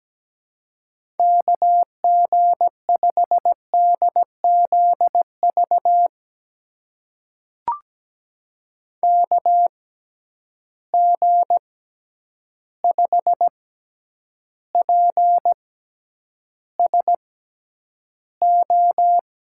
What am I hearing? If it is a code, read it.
Morse code, heard as a keyed tone, about 17 wpm, spaced as for 9 wpm, 700 Hz. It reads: KG5DZV KG5PSO
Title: USA Call Signs - 6 Characters - 6x Character Spacing - 17wpm